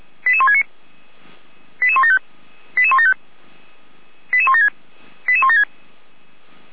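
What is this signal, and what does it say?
Five-tone selective call (ZVEI), heard as quick runs of five tones, tones of about 70 ms. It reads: 88169 88175 88175 88176 88176
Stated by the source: ZVEI five-tone call 88166 received by SDR.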